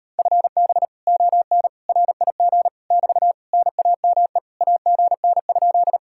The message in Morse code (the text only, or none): FB ON RIG <BT> NAME AGN?